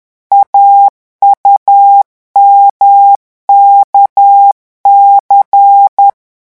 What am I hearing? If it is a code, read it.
Morse code, heard as a keyed tone, about 10 wpm, 800 Hz. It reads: AUMKC
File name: morse mp3 / revision3-vitesse 10 mots